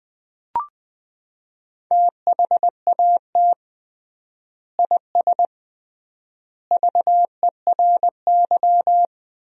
Morse code, heard as a keyed tone, about 21 wpm, 700 Hz. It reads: THAT IS VERY
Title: 3x Word-Spacing - Sentences from Top 400 Words - 20wpm